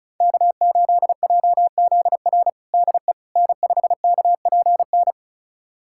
Morse code, heard as a keyed tone, about 34 wpm, 700 Hz. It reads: K8JZR DE N5KPN
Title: USA Callsign DE Callsign - 5 Characters - 35wpm